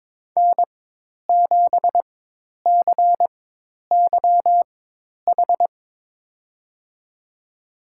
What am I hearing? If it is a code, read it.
Morse code, heard as a keyed tone, about 22 wpm, spaced as for 10 wpm, 700 Hz. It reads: N7CYH
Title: USA Call Signs - 4x Character Spacing - 22wpm